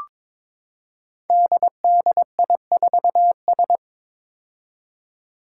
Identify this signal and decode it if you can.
Morse code, heard as a keyed tone, about 21 wpm, 700 Hz. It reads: DDI4S
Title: Five Challenging Letter-Number Combinations 22wpm